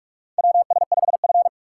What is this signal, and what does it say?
Morse code, heard as a keyed tone, about 46 wpm, 700 Hz. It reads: WS5F
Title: USA Call Signs - 3x Character Spacing - 45wpm